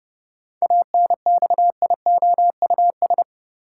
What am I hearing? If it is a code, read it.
Morse code, heard as a keyed tone, about 31 wpm, 700 Hz. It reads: ANXIOUS
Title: Seven Letter Words - 30wpm